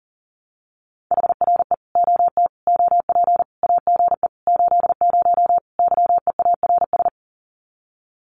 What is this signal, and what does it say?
Morse code, heard as a keyed tone, about 37 wpm, 700 Hz. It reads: HRE OT OP AGE 80 YEARS
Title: Common QSO Phrases 40wpm